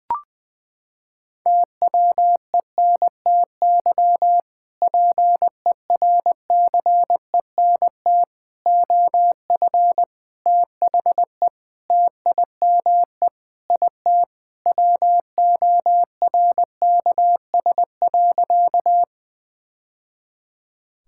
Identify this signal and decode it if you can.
Morse code, heard as a keyed tone, about 20 wpm, 700 Hz. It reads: TWENTY PERCENT OF THE TIME IT WORKS.
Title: Sentences with 2-Word Phrases - 20wpm